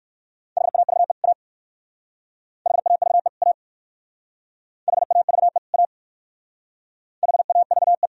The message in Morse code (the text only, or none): HAVE A HAVE A HAVE A HAVE